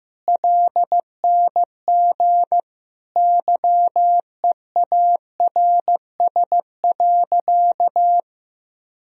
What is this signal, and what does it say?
Morse code, heard as a keyed tone, about 15 wpm, 700 Hz. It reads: LNG YEARS.
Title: Sentences from Top 200 Words and Abbreviations 15wpm